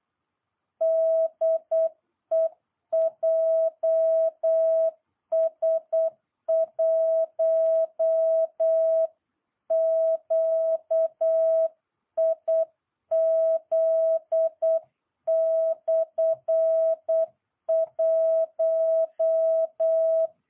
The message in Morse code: DE JS1QIZ/1

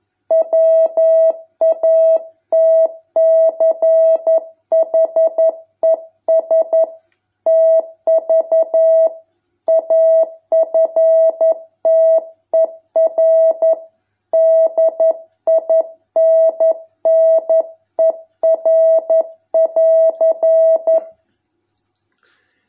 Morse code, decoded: WATCHES TV AFTER DINNER<AR>

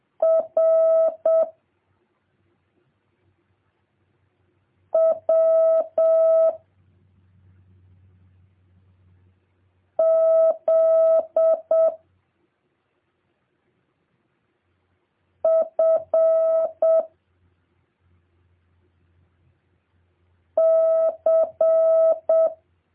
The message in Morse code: R W Z F C

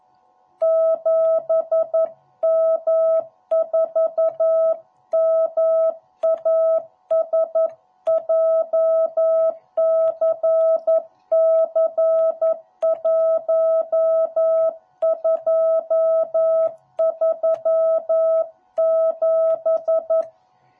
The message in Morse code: 7M4MAS JCC1237

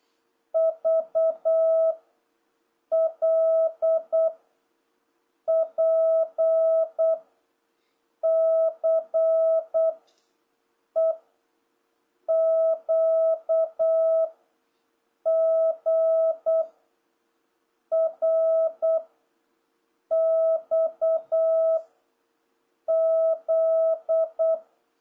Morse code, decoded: V L P C E Q G R X Z

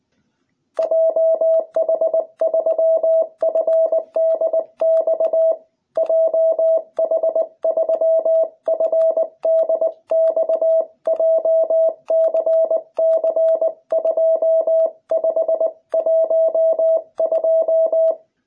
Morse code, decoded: JH3FDX JH3FDX JCC2512

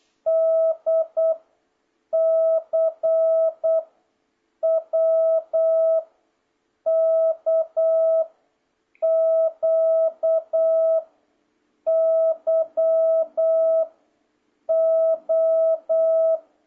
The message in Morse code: D C W K Q Y O

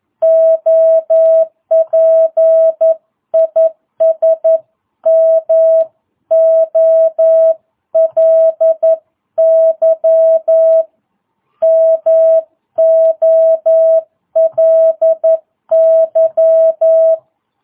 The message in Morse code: OP IS MOLY MOLY

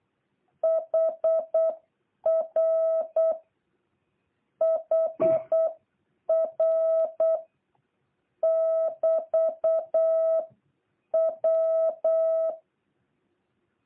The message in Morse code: HR HR <BT> W

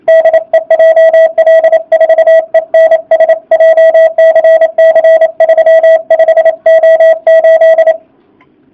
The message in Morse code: DE JL4ENS JCC 35O8

